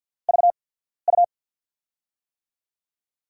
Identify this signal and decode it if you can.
Morse code, heard as a keyed tone, about 53 wpm, spaced as for 14 wpm, 700 Hz. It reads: VU
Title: USA Call Signs - 6 Characters - 8x Character Spacing - 50wpm